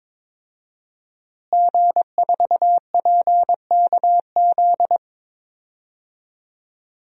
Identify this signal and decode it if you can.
Morse code, heard as a keyed tone, about 23 wpm, 700 Hz. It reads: G4PKZ
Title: DXCC-100 Call Signs - 5 Characters - Repeated 3x - 22wpm